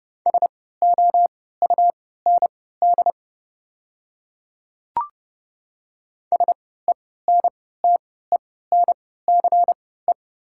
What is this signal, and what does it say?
Morse code, heard as a keyed tone, about 30 wpm, spaced as for 17 wpm, 700 Hz. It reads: SOUND SENTENCE
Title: 3x Character Spacing - Top 300 Words - 30wpm